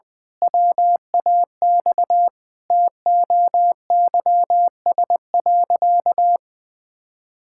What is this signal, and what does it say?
Morse code, heard as a keyed tone, about 21 wpm, 700 Hz. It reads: WAX TOYS.
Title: Pangram Challenge v2 - 20wpm